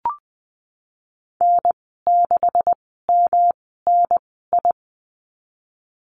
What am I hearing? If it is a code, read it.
Morse code, heard as a keyed tone, about 20 wpm, spaced as for 15 wpm, 700 Hz. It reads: N6MNI
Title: 128 Lesson 21g - Number 6 - Call Signs 20wpm